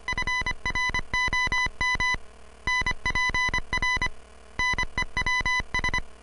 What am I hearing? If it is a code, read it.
Morse code, heard as a keyed tone, about 25 wpm, 2000 Hz. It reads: FROM NPR NEWS